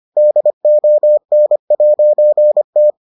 Morse code, heard as a keyed tone, about 25 wpm, 600 Hz. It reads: DON'T